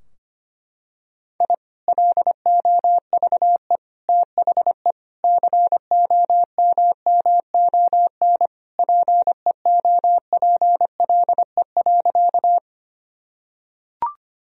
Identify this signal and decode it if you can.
Morse code, heard as a keyed tone, about 25 wpm, 700 Hz. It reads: I LOVE THE COMMON PEOPLE.
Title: Sentences from Top 1000 Words - Review Part 3 - 25wpm